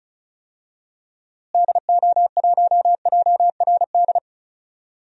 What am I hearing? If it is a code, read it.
Morse code, heard as a keyed tone, about 36 wpm, 700 Hz. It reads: DO1JRD
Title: DXCC-100 Call Signs - 6 Characters - Repeated 3x - 35wpm